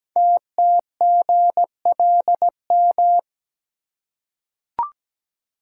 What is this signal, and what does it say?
Morse code, heard as a keyed tone, about 17 wpm, 700 Hz. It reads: TTGLM
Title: Code Groups - 5 Letters - 17wpm